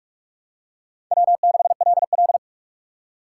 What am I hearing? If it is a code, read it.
Morse code, heard as a keyed tone, about 44 wpm, 700 Hz. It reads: W6LL
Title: USA Call Signs - 4 Characters - 6x Character Spacing - 45wpm